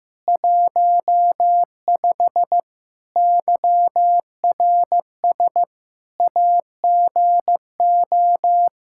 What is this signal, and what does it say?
Morse code, heard as a keyed tone, about 15 wpm, 700 Hz. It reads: 15 YRS AGO
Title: Practice QSO - Exchange 3 - More Info - 15wpm